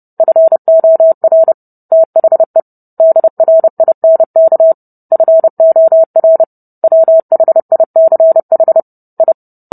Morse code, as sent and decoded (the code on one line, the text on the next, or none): ..-. --- .-. / - .... / -.. .-. .. -. -.- / ..-. --- .-. / .-- .... .. -.-. .... / ..
FOR THE DRINK FOR WHICH I